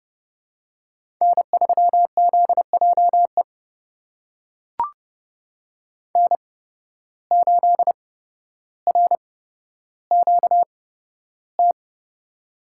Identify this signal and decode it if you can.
Morse code, heard as a keyed tone, about 29 wpm, spaced as for 13 wpm, 700 Hz. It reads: N3ZJE N8RQT
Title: USA Call Signs - 5 Characters - 8x Character Spacing - 30wpm